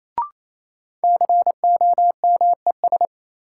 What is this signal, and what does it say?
Morse code, heard as a keyed tone, about 28 wpm, 700 Hz. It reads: COMES
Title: Sentences from Top 1000 Words - Review Part 5 - New Word Drill - 28wpm